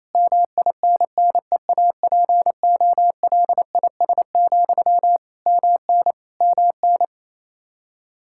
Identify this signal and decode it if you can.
Morse code, heard as a keyed tone, about 28 wpm, 700 Hz. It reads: MINNEAPOLIS, MN MN
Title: Practice QSO - Exchange 1 - Beginning - 28wpm